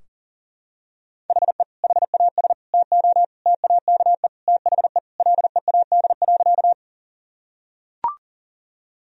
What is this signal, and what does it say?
Morse code, heard as a keyed tone, about 40 wpm, 700 Hz. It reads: HE HAS TO TAKE THE LEAD.
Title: Sentences from Top 1000 Words - Review Part 2 - 40wpm